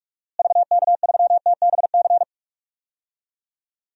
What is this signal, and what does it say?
Morse code, heard as a keyed tone, about 42 wpm, 700 Hz. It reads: VK3TBC
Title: DXCC-100 Call Signs - 6 Characters - Repeated 3x - 45wpm